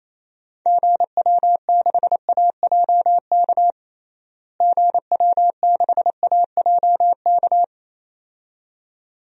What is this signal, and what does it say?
Morse code, heard as a keyed tone, about 27 wpm, 700 Hz. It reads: GW6AJK GW6AJK
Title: DXCC-100 Call Signs - 6 Characters - Repeated 2x - 28wpm